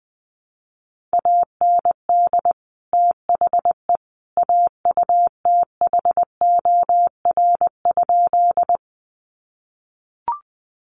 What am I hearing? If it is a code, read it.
Morse code, heard as a keyed tone, about 19 wpm, 700 Hz. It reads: AND THE AUTHOR?
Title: Sentences from Top 1200 Words 20wpm